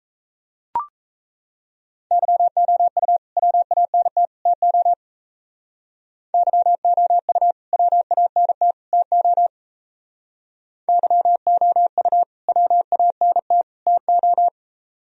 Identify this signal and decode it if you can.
Morse code, heard as a keyed tone, about 35 wpm, 700 Hz. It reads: YOU WANT TO YOU WANT TO YOU WANT TO